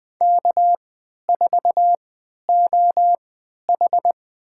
Morse code, heard as a keyed tone, about 20 wpm, spaced as for 11 wpm, 700 Hz. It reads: K4OH
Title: USA Call Signs - 3x Character Spacing - 20wpm